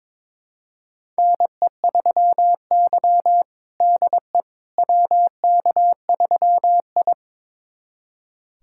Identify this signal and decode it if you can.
Morse code, heard as a keyed tone, about 22 wpm, 700 Hz. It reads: NE3Y DE WK3I
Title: USA Callsign DE Callsign - 4 Characters - 22wpm